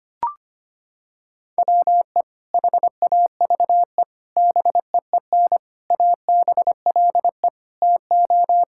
Morse code, heard as a keyed tone, about 25 wpm, 700 Hz. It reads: WE HAVE BEEN ABLE TO
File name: practice / 25wpm / Sentences with 4-Word Phrases - 25wpm